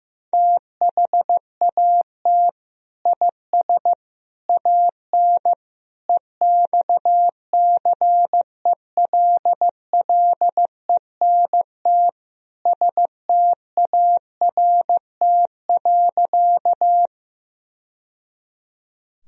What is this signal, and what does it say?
Morse code, heard as a keyed tone, about 16 wpm, 700 Hz. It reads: THAT IS AN EXCELLENT START.
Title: Sentences from Top 1000 Words - Review Part 1 - 15wpm